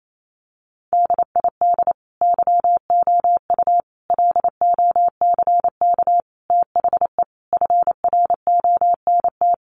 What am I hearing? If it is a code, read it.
Morse code, heard as a keyed tone, about 28 wpm, 700 Hz. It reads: DID YOU LOCK THE FRONT
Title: Sentences from Top 1300 Words 28wpm